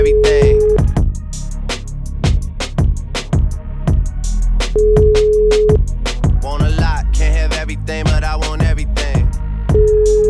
Ringback tone (Europe, 425 Hz).